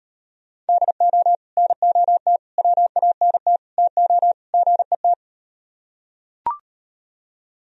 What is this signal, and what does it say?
Morse code, heard as a keyed tone, about 38 wpm, 700 Hz. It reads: DO NOT WANT TO GET